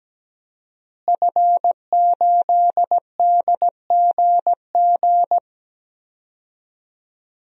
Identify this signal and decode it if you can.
Morse code, heard as a keyed tone, about 17 wpm, 700 Hz. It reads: F8DGG
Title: DXCC-100 Call Signs - 5 Characters - Repeated 3x - 17wpm